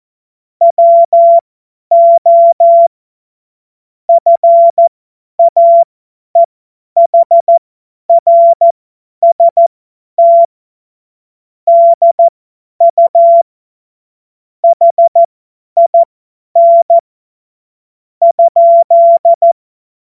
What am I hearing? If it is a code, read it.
Morse code, heard as a keyed tone, about 14 wpm, spaced as for 10 wpm, 680 Hz. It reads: WO FAEHRST DU HIN ?